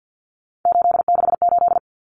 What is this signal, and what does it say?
Morse code, heard as a keyed tone, about 50 wpm, 700 Hz. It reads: 868
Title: Numbers - 3-Digit 50wpm